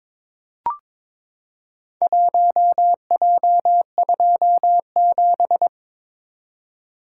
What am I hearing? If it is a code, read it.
Morse code, heard as a keyed tone, about 22 wpm, 700 Hz. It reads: 1J27
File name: practice / 22wpm / Four Letter-Number Combinations - 22wpm